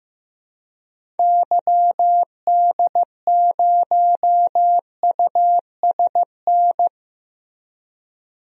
Morse code, heard as a keyed tone, about 15 wpm, 700 Hz. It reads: YD0USN
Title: DXCC-100 Call Signs - 6 Characters - Repeated 3x - 15wpm